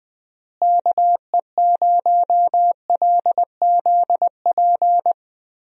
Morse code, heard as a keyed tone, about 20 wpm, 700 Hz. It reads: KE0LZP
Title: USA Call Signs - 6 Characters - Repeated 3x - 20wpm